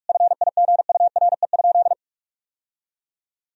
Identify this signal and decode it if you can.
Morse code, heard as a keyed tone, about 46 wpm, 700 Hz. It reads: FIGURE?
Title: Sentences from Top 1000 Words - Review Part 2 - 45wpm